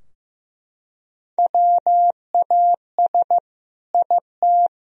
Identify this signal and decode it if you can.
Morse code, heard as a keyed tone, about 15 wpm, 700 Hz. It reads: WAS IT
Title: Sentences from Top 1000 Words - Review Part 4 - 15wpm